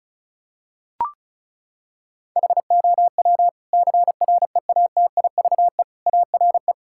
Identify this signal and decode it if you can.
Morse code, heard as a keyed tone, about 34 wpm, 700 Hz. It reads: HOW CREATIVE ARE
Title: Sentences from Top 1000 Words - Review Part 5 - 35wpm